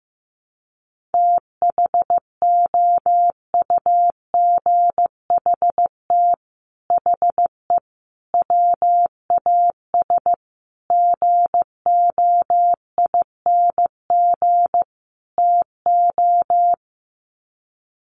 Morse code, heard as a keyed tone, about 15 wpm, 700 Hz. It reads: THOUGHT HE WAS GOING TO
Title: Sets of 5 Words Encore2 15wpm